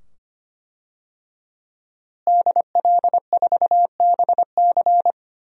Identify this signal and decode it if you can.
Morse code, heard as a keyed tone, about 25 wpm, 700 Hz. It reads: DL4BC
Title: DXCC-100 Call Signs - 5 Characters - Repeated 2x - 25wpm